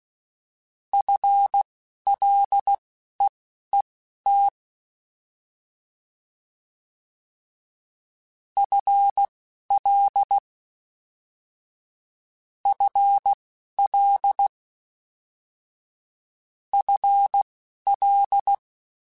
Morse code, heard as a keyed tone, about 16 wpm, spaced as for 11 wpm, 800 Hz. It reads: FLEET FL FL FL